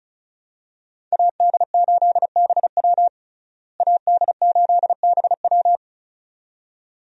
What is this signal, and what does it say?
Morse code, heard as a keyed tone, about 34 wpm, 700 Hz. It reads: AD8BW AD8BW
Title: USA Call Signs - 5 Characters - Repeated 2x - 35wpm